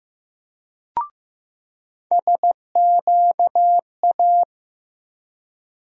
Morse code, heard as a keyed tone, about 15 wpm, 700 Hz. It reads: SQA
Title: USA Call Sign Postfixes - 3 Letters 15wpm